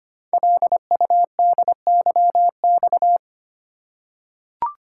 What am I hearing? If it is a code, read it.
Morse code, heard as a keyed tone, about 25 wpm, 700 Hz. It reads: LUDYX
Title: Code Groups - 5 Letters - 25wpm